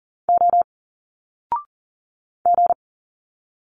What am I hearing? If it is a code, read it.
Morse code, heard as a keyed tone, about 38 wpm, 700 Hz. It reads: O G
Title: Single Letters 40wpm